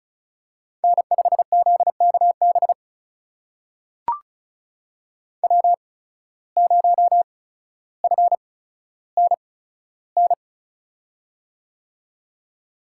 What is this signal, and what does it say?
Morse code, heard as a keyed tone, about 36 wpm, spaced as for 15 wpm, 700 Hz. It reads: N5ZKB W0FNN
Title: USA Call Signs - 5 Characters - 8x Character Spacing - 35wpm